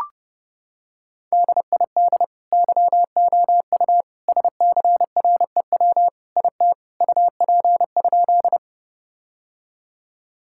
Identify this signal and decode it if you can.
Morse code, heard as a keyed tone, about 32 wpm, 700 Hz. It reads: DID YOU SCREW IT UP?
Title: Sentences from Top 1600 Words 30wpm